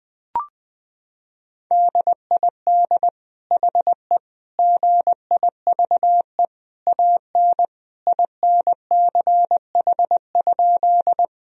Morse code, heard as a keyed tone, about 20 wpm, 700 Hz. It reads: DID HE GIVE AN INCH?
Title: Sentences from Top 1000 Words - Review Part 3 - 20wpm